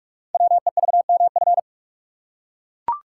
Morse code, heard as a keyed tone, about 44 wpm, 700 Hz. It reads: WEVMF
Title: Code Groups - 5 Letters - 45wpm